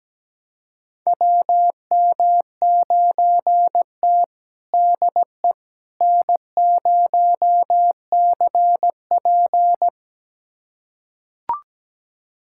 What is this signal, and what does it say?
Morse code, heard as a keyed tone, about 17 wpm, 700 Hz. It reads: WM9T DE N0CP
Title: USA Callsign DE Callsign - 4 Characters - 17wpm